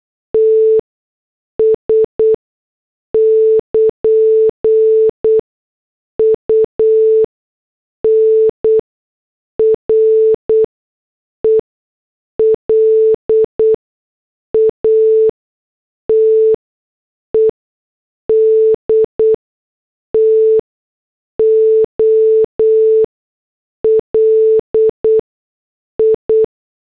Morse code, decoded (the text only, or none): TS <KN> UNRELATED TO LI